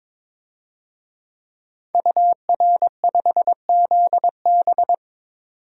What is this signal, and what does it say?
Morse code, heard as a keyed tone, about 22 wpm, 700 Hz. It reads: UR5ZB
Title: DXCC-100 Call Signs - 5 Characters - 22wpm